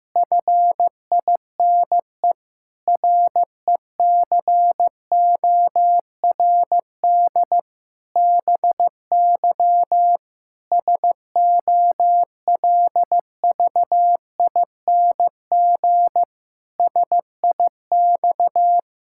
Morse code, heard as a keyed tone, about 15 wpm, 700 Hz. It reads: FINE RECORD BY SOLVING SIX
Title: Pangram Challenge v2 - 15wpm